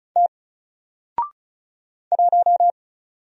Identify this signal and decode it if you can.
Morse code, heard as a keyed tone, about 35 wpm, 700 Hz. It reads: T 1
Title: Single Letter-Number 35wpm